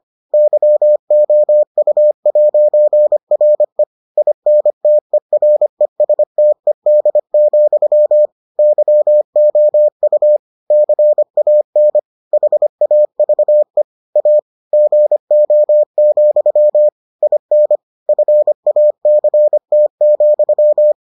Morse code, heard as a keyed tone, about 25 wpm, 600 Hz. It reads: YOU'RE INTERESTED, YOU CAN HAVE A GO, IN FACT,